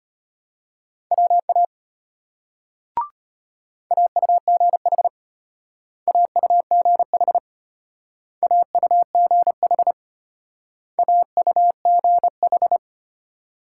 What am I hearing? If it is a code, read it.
Morse code, heard as a keyed tone, about 32 wpm, 700 Hz. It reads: WA AUGH AUGH AUGH AUGH